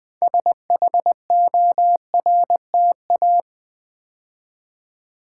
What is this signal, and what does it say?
Morse code, heard as a keyed tone, about 20 wpm, 700 Hz. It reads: SHORTA
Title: Build A Word - 3 Letters At A Time - 8-Letter Words - 20wpm